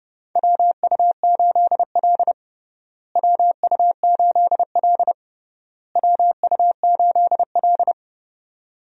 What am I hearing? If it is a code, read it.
Morse code, heard as a keyed tone, about 30 wpm, 700 Hz. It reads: WU8L WU8L WU8L
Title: USA Call Signs - 4 Characters - Repeated 3x - 30wpm